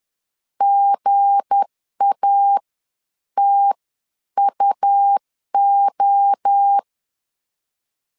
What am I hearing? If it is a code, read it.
Morse code, heard as a keyed tone, about 10 wpm, spaced as for 9 wpm, 800 Hz. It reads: GATUO
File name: LEZIONI / 7 GF9Z / 7 GF 10wpm 2514